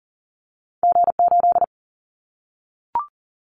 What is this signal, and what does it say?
Morse code, heard as a keyed tone, about 40 wpm, 700 Hz. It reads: G8
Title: Callsign Prefixes for Top 25 DXCC - 40wpm